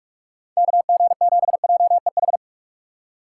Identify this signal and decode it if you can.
Morse code, heard as a keyed tone, about 46 wpm, 700 Hz. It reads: KG7JEH
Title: USA Call Signs - 6 Characters - 4x Character Spacing - 45wpm